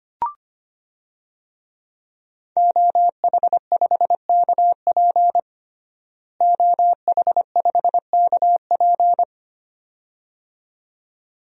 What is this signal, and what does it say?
Morse code, heard as a keyed tone, about 25 wpm, 700 Hz. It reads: OH5KP OH5KP
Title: DXCC-100 Call Signs - 5 Characters - Repeated 2x - 25wpm